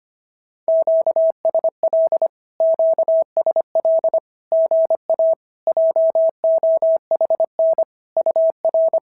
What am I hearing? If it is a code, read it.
Morse code, heard as a keyed tone, about 25 wpm, 690 Hz. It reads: QSL QSL GA JOHN UR
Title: SOTA QSO Practice 25wpm